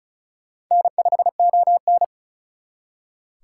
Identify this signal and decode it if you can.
Morse code, heard as a keyed tone, about 48 wpm, spaced as for 42 wpm, 700 Hz. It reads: N5ON
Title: Speed Racing - USA Call Signs 35wpm